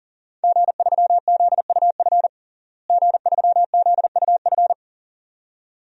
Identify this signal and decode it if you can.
Morse code, heard as a keyed tone, about 40 wpm, 700 Hz. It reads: G3ZUF G3ZUF
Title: DXCC-100 Call Signs - 5 Characters - Repeated 2x - 40wpm